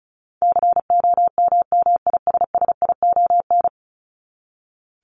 Morse code, heard as a keyed tone, about 34 wpm, 700 Hz. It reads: COMMISSION
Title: Top 1251-1500 Words 35wpm